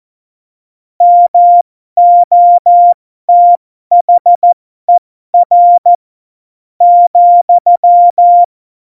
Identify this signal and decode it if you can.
Morse code, heard as a keyed tone, about 14 wpm, spaced as for 12 wpm, 700 Hz. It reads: MOTHER ,
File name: Morsekurs / Witze